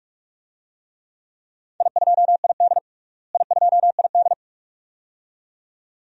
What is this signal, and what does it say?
Morse code, heard as a keyed tone, about 43 wpm, 700 Hz. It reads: I2ID I2ID
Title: DXCC-100 Call Signs - 4 Characters - Repeated 2x - 45wpm